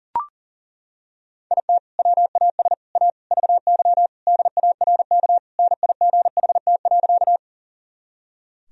Morse code, heard as a keyed tone, about 40 wpm, 700 Hz. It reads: IT WAS A VY DARK NIGHT.
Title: Sentences from Top 500 Words and Abbreviations 40wpm